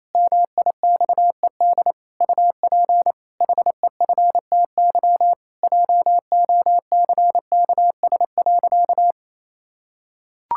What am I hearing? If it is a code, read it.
Morse code, heard as a keyed tone, about 28 wpm, 700 Hz. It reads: MIXED UP HEFTY JOCKS.
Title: Pangram Challenge v2 - 28wpm